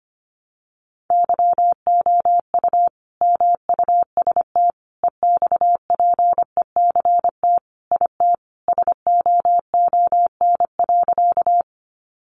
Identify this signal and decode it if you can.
Morse code, heard as a keyed tone, about 25 wpm, 700 Hz. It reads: YOU MUST EXPECT IT SOON.
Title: Sentences from Top 1100 Words 25wpm